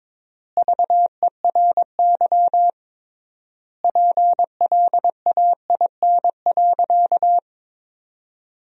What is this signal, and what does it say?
Morse code, heard as a keyed tone, about 22 wpm, 700 Hz. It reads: VERY PLAIN.
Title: 3x Word-Spacing - Sentences from Top 400 Words - 22wpm